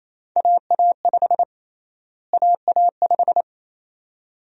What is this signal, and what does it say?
Morse code, heard as a keyed tone, about 27 wpm, 700 Hz. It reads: AA5 AA5
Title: USA Call Sign Prefixes - Repeated 2x - 28wpm